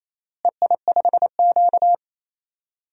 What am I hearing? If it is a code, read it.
Morse code, heard as a keyed tone, about 28 wpm, 700 Hz. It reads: EI5Q
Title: DXCC-100 Call Signs - 4 Characters - Repeated 3x - 28wpm